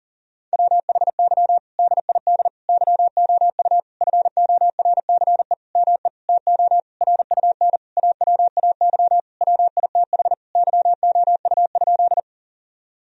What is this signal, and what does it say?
Morse code, heard as a keyed tone, about 40 wpm, 700 Hz. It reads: WHY DID YOU FORCE ME TO RUN AWAY WITH YOU?
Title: Sentences from Top 1000 Words - Review Part 3 - 40wpm